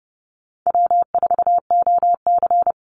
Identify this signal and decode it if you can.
Morse code, heard as a keyed tone, about 30 wpm, 700 Hz. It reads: W4OC
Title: Easy Call Signs - 30wpm